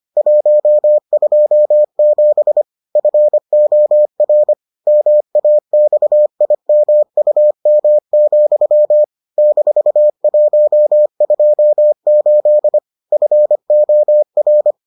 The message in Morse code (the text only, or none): 127 FOR MAXIMUM, -128 FOR